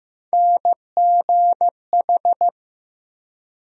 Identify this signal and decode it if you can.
Morse code, heard as a keyed tone, about 15 wpm, 700 Hz. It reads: NGH
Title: USA Call Sign Postfixes - 3 Letters 15wpm